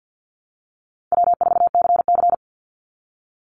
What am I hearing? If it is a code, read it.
Morse code, heard as a keyed tone, about 50 wpm, 700 Hz. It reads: W4CC
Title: Call Signs - 50wpm